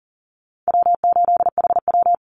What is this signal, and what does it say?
Morse code, heard as a keyed tone, about 40 wpm, 700 Hz. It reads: W8HW
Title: Call Signs - 40wpm